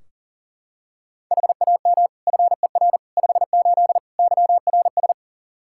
Morse code, heard as a keyed tone, about 40 wpm, 700 Hz. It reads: HAM FER 58 YRS
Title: Practice QSO - Exchange 3 - More Info - 40wpm